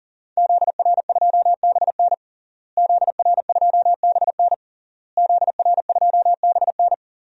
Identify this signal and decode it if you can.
Morse code, heard as a keyed tone, about 40 wpm, 700 Hz. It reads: ZR2BN ZR2BN ZR2BN
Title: DXCC-100 Call Signs - 5 Characters - Repeated 3x - 40wpm